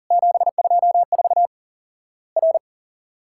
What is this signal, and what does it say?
Morse code, heard as a keyed tone, about 42 wpm, 700 Hz. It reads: 724 R
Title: Worked All Europe DX Contest - QTC Exchange - 40wpm